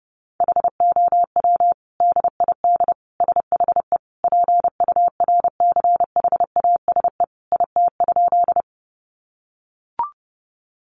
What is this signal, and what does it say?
Morse code, heard as a keyed tone, about 30 wpm, 700 Hz. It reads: HOW DID SHE PURCHASE IT?
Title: Sentences from Top 800 Words 30wpm